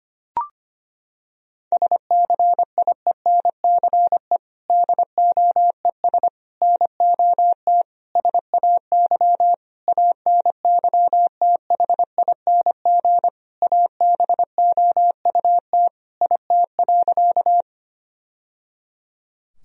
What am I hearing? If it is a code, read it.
Morse code, heard as a keyed tone, about 25 wpm, 700 Hz. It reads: SCIENCE DOES NOT SAY ANYTHING ABOUT IT.
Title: Sentences from Top 1000 Words - Review Part 2 - 25wpm